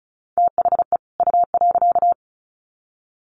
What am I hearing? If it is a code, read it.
Morse code, heard as a keyed tone, about 34 wpm, 700 Hz. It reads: THE U.
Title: Sets of 2 Words 35wpm